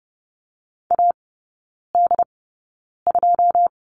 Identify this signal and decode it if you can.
Morse code, heard as a keyed tone, about 30 wpm, spaced as for 9 wpm, 700 Hz. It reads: AD2
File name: 081 Lesson 14e - Number 2 - Three Characters 30wpm